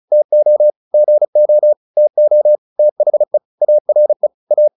Morse code, heard as a keyed tone, about 34 wpm, 600 Hz. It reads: TO GO TO THE ARE A